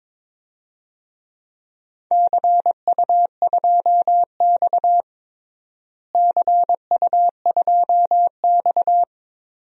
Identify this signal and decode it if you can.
Morse code, heard as a keyed tone, about 22 wpm, 700 Hz. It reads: CU2X CU2X
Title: DXCC-100 Call Signs - 4 Characters - Repeated 3x - 22wpm